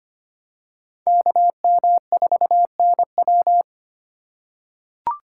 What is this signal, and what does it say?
Morse code, heard as a keyed tone, about 25 wpm, 700 Hz. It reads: KM4NW
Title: USA Call Signs - 4x Character Spacing - 25wpm